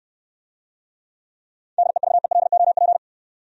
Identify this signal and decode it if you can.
Morse code, heard as a keyed tone, about 68 wpm, 700 Hz. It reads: DEVELOP